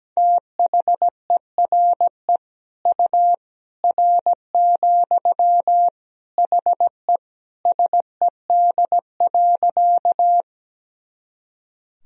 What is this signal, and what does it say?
Morse code, heard as a keyed tone, about 17 wpm, 700 Hz. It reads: THERE U R, HE SED.
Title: Sentences from Top 100 Words and Abbreviations 17wpm